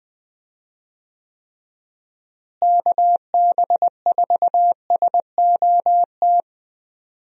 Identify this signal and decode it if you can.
Morse code, heard as a keyed tone, about 21 wpm, 700 Hz. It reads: KB4SOT
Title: USA Call Signs - 6 Characters - Repeated 2x - 20wpm